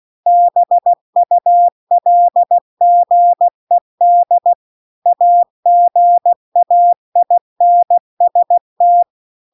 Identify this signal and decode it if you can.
Morse code, heard as a keyed tone, about 16 wpm, 700 Hz. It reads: BULGED AGAINST
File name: CW-The-Big-Twoheaded-River-2-16WPM